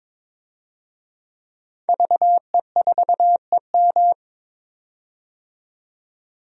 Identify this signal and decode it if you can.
Morse code, heard as a keyed tone, about 22 wpm, 700 Hz. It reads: VE4EM
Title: DXCC-100 Call Signs - 5 Characters - 22wpm